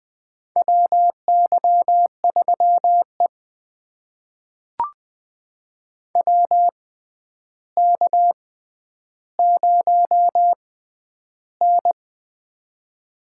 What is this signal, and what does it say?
Morse code, heard as a keyed tone, about 20 wpm, spaced as for 10 wpm, 700 Hz. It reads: WY3E WK0N
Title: USA Call Signs - 4 Characters - 6x Character Spacing - 20wpm